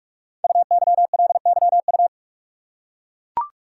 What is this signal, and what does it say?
Morse code, heard as a keyed tone, about 44 wpm, 700 Hz. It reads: UYLYU
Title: Code Groups - 5 Letters - 45wpm